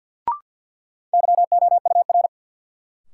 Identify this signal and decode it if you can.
Morse code, heard as a keyed tone, about 52 wpm, 700 Hz. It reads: YOUR
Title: Sets of 2 Words - New Word Drill - 50wpm